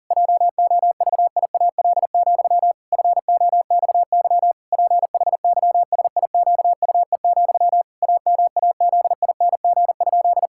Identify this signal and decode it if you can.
Morse code, heard as a keyed tone, about 40 wpm, 700 Hz. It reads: JOVIAL, FOXY PHYSIQUE, AMAZING?
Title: Pangram Challenge v2 - 40wpm